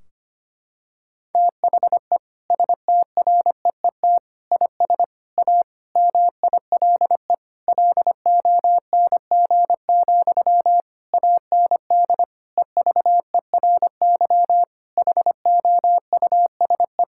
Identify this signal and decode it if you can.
Morse code, heard as a keyed tone, about 25 wpm, 700 Hz. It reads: THE STREET IS A MILE LONG, AND EVERY HOUSE
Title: Sentences from Top 1600 Words 25wpm